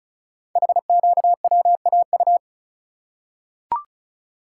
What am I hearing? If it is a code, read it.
Morse code, heard as a keyed tone, about 36 wpm, 700 Hz. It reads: HQWAU
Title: Code Groups - 5 Letters - 35wpm